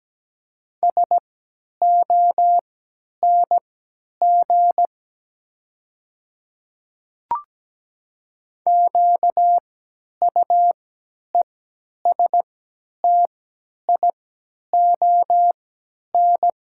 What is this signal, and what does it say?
Morse code, heard as a keyed tone, about 17 wpm, spaced as for 10 wpm, 690 Hz. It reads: SONG QUESTION
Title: 3x Character Spacing - Top 500 Words - 17wpm